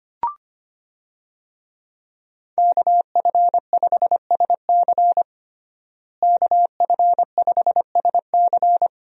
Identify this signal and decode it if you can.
Morse code, heard as a keyed tone, about 25 wpm, 700 Hz. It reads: KF5SC KF5SC
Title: USA Call Signs - 5 Characters - Repeated 2x - 25wpm